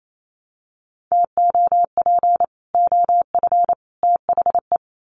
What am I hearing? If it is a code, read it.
Morse code, heard as a keyed tone, about 28 wpm, 700 Hz. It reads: TOP OF THE
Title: Sets of 3 Words 28wpm